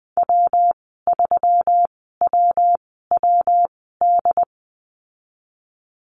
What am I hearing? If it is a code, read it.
Morse code, heard as a keyed tone, about 20 wpm, spaced as for 15 wpm, 700 Hz. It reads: W3WWD
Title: 128 Lesson 21g - Number 6 - Call Signs 20wpm